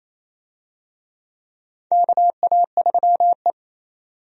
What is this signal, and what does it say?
Morse code, heard as a keyed tone, about 28 wpm, 700 Hz. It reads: KA3E
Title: USA Call Signs - 4 Characters - 28wpm